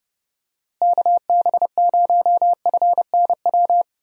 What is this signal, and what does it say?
Morse code, heard as a keyed tone, about 30 wpm, 700 Hz. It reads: KB0FNW
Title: USA Call Signs - 6 Characters - Repeated 2x - 30wpm